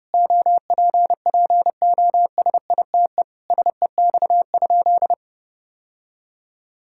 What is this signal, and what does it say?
Morse code, heard as a keyed tone, about 30 wpm, 700 Hz. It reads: OPPOSITE SEX?
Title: Sentences from Top 1600 Words 30wpm